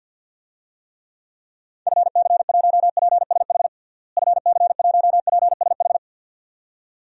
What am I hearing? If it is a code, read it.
Morse code, heard as a keyed tone, about 47 wpm, 700 Hz. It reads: FC1PSH FC1PSH
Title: DXCC-100 Call Signs - 6 Characters - Repeated 2x - 50wpm